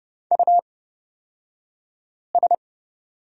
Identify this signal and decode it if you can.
Morse code, heard as a keyed tone, about 30 wpm, spaced as for 5 wpm, 700 Hz. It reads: US
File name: Lesson 36d - BT - Two Characters 30wpm